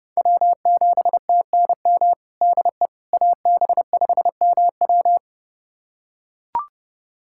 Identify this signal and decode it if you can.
Morse code, heard as a keyed tone, about 30 wpm, 700 Hz. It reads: W7TNM DE AB5MW
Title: USA Callsign DE Callsign - 5 Characters - 30wpm